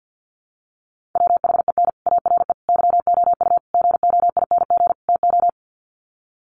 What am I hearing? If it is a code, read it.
Morse code, heard as a keyed tone, about 50 wpm, 700 Hz. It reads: WHEN ARE YOU GOING TO
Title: Sets of 5 Words Encore2 50wpm